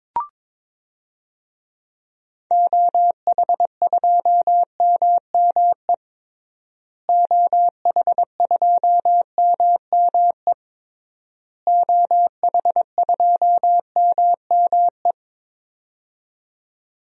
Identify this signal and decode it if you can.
Morse code, heard as a keyed tone, about 22 wpm, 700 Hz. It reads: OH2MME OH2MME OH2MME
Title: DXCC-100 Call Signs - 6 Characters - Repeated 3x - 22wpm